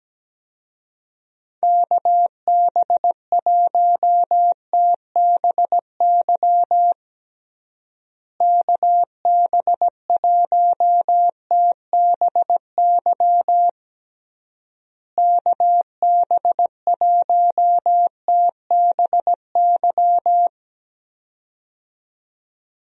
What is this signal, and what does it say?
Morse code, heard as a keyed tone, about 17 wpm, 700 Hz. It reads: KB1TBY KB1TBY KB1TBY
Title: USA Call Signs - 6 Characters - Repeated 3x - 17wpm